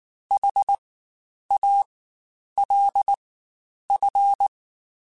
Morse code, heard as a keyed tone, about 19 wpm, spaced as for 9 wpm, 800 Hz. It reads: HALF
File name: Spaced / Chuckles Spaced-mp3s / HalfFull sp @18